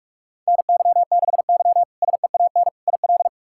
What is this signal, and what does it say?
Morse code, heard as a keyed tone, about 46 wpm, 700 Hz. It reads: NY6Y SEAN IL
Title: K1USN SST QSO Practice - 45wpm